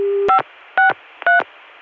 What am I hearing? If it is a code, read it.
DTMF, heard as telephone touch tones, tones of about 130 ms. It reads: 563